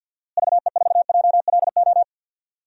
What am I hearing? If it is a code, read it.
Morse code, heard as a keyed tone, about 50 wpm, 700 Hz. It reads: VE4JLO
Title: DXCC-100 Call Signs - 6 Characters - Repeated 2x - 50wpm